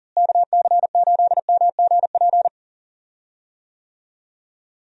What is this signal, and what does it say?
Morse code, heard as a keyed tone, about 37 wpm, 700 Hz. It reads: KC8MGP
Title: USA Call Signs - 6 Characters - Repeated 2x - 40wpm